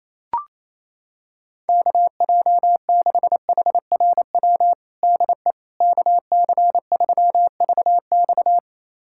Morse code, heard as a keyed tone, about 28 wpm, 700 Hz. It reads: KJ6HRW DE KC3VX
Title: USA Callsign DE Callsign 28wpm